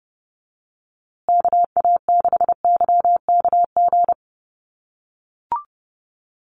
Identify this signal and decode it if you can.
Morse code, heard as a keyed tone, about 30 wpm, 700 Hz. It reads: KA6YKG
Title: USA Call Signs Repeated 2x 30wpm